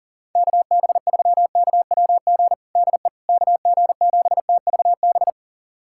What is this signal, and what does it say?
Morse code, heard as a keyed tone, about 37 wpm, 700 Hz. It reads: KB3KWG DE KG7TVB
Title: USA Callsign DE Callsign 40wpm